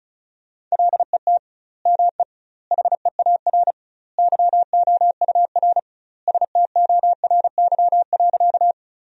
Morse code, heard as a keyed tone, about 36 wpm, 700 Hz. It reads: LET ME HEAR YOUR STORY.
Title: 2x Word-Spacing - Sentences from Top 300 Words - 35wpm